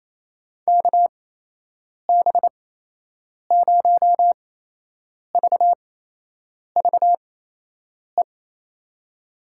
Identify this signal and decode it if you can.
Morse code, heard as a keyed tone, about 28 wpm, spaced as for 8 wpm, 700 Hz. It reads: KB0VVE
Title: USA Call Signs - 6 Characters - 8x Character Spacing - 28wpm